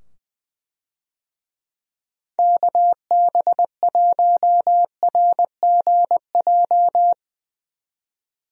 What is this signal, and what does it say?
Morse code, heard as a keyed tone, about 20 wpm, 700 Hz. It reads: KB1RGJ
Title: USA Call Signs - 6 Characters - 20wpm